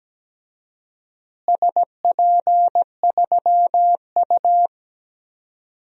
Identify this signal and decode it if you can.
Morse code, heard as a keyed tone, about 17 wpm, 700 Hz. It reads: SP3U
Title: DXCC-100 Call Signs - 4 Characters - Repeated 2x - 17wpm